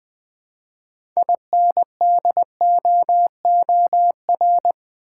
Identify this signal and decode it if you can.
Morse code, heard as a keyed tone, about 20 wpm, 700 Hz. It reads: INDOOR
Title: Build A Word - 3 Letters At A Time - 6-Letter Words - 20wpm